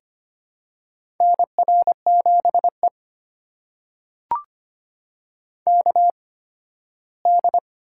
Morse code, heard as a keyed tone, about 25 wpm, spaced as for 15 wpm, 700 Hz. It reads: NR7E KD
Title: USA Call Signs - 4 Characters - 8x Character Spacing - 25wpm